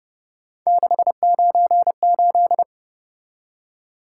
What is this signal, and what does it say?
Morse code, heard as a keyed tone, about 31 wpm, 700 Hz. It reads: 698
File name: Speed-Racing - Three Letter-Number Combinations - 25wpm